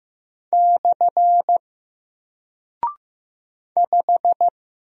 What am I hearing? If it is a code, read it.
Morse code, heard as a keyed tone, about 15 wpm, 700 Hz. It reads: / 5
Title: Single Character (Letter:Number:Punctuation:Prosign) 15wpm